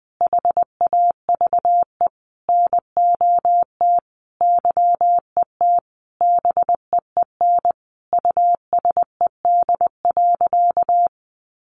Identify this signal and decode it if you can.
Morse code, heard as a keyed tone, about 20 wpm, 700 Hz. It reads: HAVE NOT YET BEEN USED.
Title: Sentences from Top 700 Words 20wpm